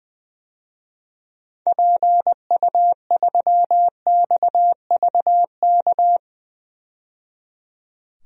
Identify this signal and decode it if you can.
Morse code, heard as a keyed tone, about 20 wpm, 700 Hz. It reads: PU3XVK
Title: DXCC-100 Call Signs - 6 Characters - 20wpm